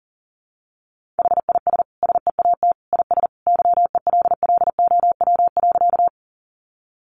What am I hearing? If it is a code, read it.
Morse code, heard as a keyed tone, about 40 wpm, 700 Hz. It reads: HIS SEAT IS YELLOW.
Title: Additional Easy Phrases 40wpm